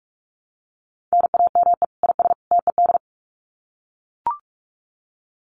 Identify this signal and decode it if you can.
Morse code, heard as a keyed tone, about 44 wpm, 700 Hz. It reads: NAME IS TED
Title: Common QSO Phrases 45wpm